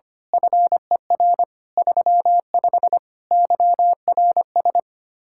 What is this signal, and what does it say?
Morse code, heard as a keyed tone, about 25 wpm, 700 Hz. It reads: FER 35 YRS
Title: Speed-Racing Common QSO Phrases - 25wpm